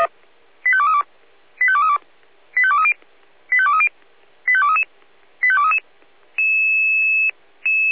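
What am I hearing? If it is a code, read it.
Five-tone selective call (ZVEI), heard as quick runs of five tones, tones of about 70 ms. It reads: 85232 85232 85230 85230 85233 85233
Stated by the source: ZVEI five-tone selective call received on an SDR.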